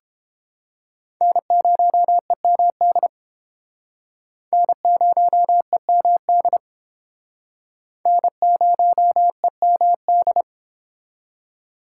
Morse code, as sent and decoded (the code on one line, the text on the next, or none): -. ----- . -- -.. / -. ----- . -- -.. / -. ----- . -- -..
N0EMD N0EMD N0EMD